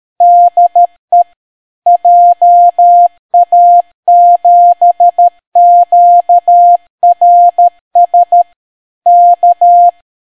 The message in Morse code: DE JA7QRS K